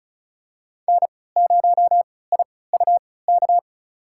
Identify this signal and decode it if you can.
Morse code, heard as a keyed tone, about 34 wpm, spaced as for 20 wpm, 700 Hz. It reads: N0IUK
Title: USA Call Signs - 5 Characters - 3x Character Spacing - 35wpm